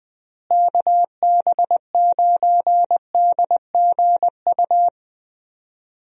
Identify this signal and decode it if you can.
Morse code, heard as a keyed tone, about 20 wpm, 700 Hz. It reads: KB9DGU
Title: USA Call Signs - 6 Characters - Repeated 3x - 20wpm